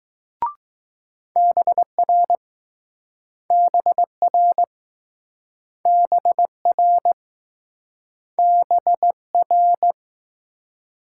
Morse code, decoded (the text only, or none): BR BR BR BR